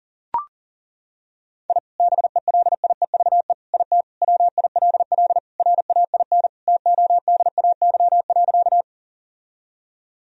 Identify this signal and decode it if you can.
Morse code, heard as a keyed tone, about 37 wpm, 700 Hz. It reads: I BELIEVE IT WILL RAIN TODAY.
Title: Sentences from Top 1000 Words - Review Part 1 - 40wpm